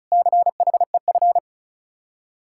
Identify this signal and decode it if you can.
Morse code, heard as a keyed tone, about 43 wpm, 700 Hz. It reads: CHEF
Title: Speed-Racing Common QSO Phrases - 35wpm